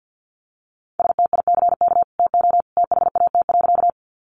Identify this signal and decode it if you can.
Morse code, heard as a keyed tone, about 50 wpm, 700 Hz. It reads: STICK TO THAT.
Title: Sentences from Top 1200 Words 50wpm